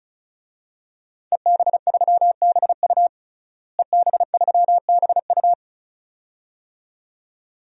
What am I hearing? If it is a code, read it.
Morse code, heard as a keyed tone, about 34 wpm, 700 Hz. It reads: EB3BU EB3BU
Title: DXCC-100 Call Signs - 5 Characters - Repeated 2x - 35wpm